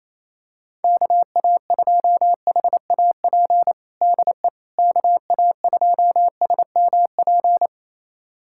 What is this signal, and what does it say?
Morse code, heard as a keyed tone, about 28 wpm, 700 Hz. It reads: KA2HAP DE KA2SMP
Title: USA Callsign DE Callsign - 6 Characters - 28wpm